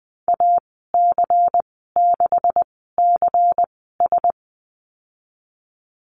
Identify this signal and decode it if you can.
Morse code, heard as a keyed tone, about 20 wpm, spaced as for 14 wpm, 700 Hz. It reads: AC6CS
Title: 161 Lesson 26g - Letter g - Call Signs 20wpm